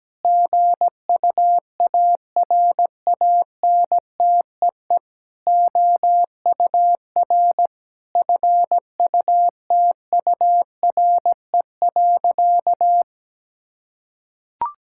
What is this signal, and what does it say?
Morse code, heard as a keyed tone, about 17 wpm, 700 Hz. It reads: GUARANTEE OUR FUTURE.
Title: Sentences from Top 1600 Words 17wpm